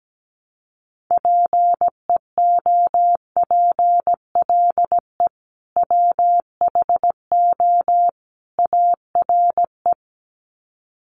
Morse code, heard as a keyed tone, about 17 wpm, 700 Hz. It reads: PEOPLE WHO ARE
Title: Sets of 3 Words 17wpm